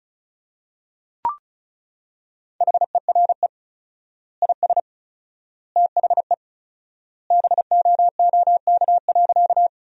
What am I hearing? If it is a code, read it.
Morse code, heard as a keyed tone, about 34 wpm, 700 Hz. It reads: HERE IS THE BOOK.
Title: 4x Word-Spacing - Sentences from Top 300 Words - 35wpm